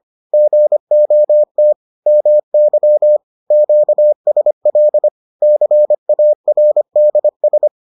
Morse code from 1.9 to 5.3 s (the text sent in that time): MY QSL